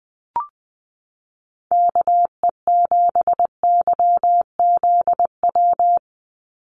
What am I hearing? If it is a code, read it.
Morse code, heard as a keyed tone, about 20 wpm, 700 Hz. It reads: KE7YZW
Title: USA Call Signs 20wpm